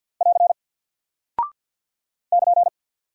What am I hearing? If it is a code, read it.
Morse code, heard as a keyed tone, about 48 wpm, 700 Hz. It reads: <AR> <KN>
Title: Single Character (Letter:Number:Punctuation:Prosign) Alt 50wpm